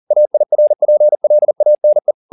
Morse code, heard as a keyed tone, about 40 wpm, 600 Hz. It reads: AIRPLANE
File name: the old man and the sea 2 40wpm